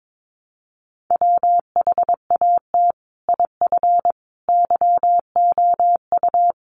Morse code, heard as a keyed tone, about 22 wpm, 700 Hz. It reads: WHAT IF YOU
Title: Sentences from Top 1100 Words 22wpm